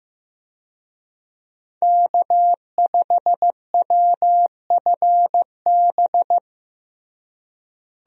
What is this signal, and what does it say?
Morse code, heard as a keyed tone, about 15 wpm, 700 Hz. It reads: K5WFB
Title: USA Call Signs - 5 Characters - Repeated 2x - 15wpm